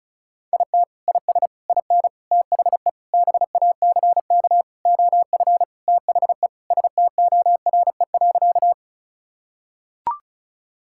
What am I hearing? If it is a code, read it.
Morse code, heard as a keyed tone, about 36 wpm, 700 Hz. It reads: IT IS IN THE BACK OF THE STORE.
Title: Sentences from Top 1000 Words - Review Part 5 - 35wpm